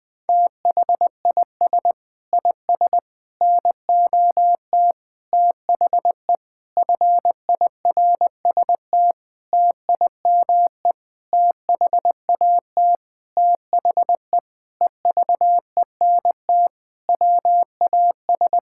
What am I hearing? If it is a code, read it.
Morse code, heard as a keyed tone, about 20 wpm, 700 Hz. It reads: THIS IS NOT THE FIRST TIME THAT THE EVENT WAS
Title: Sentences with 8-Word Phrases - 20wpm